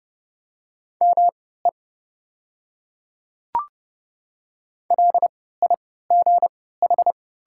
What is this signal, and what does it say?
Morse code, heard as a keyed tone, about 30 wpm, spaced as for 17 wpm, 700 Hz. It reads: ME LIGH
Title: 3x Character Spacing - Top 200 Words - 30wpm